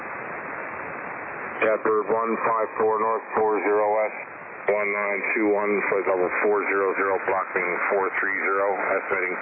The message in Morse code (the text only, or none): none